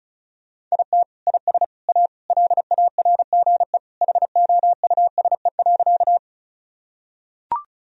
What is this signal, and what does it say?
Morse code, heard as a keyed tone, about 36 wpm, 700 Hz. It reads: IT IS A LARGE HOUSE.
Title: Sentences from Top 1000 Words - Review Part 1 - 35wpm